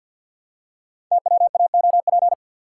Morse code, heard as a keyed tone, about 51 wpm, 700 Hz. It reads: TWAOP
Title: Code Groups - 5 Letters - 50wpm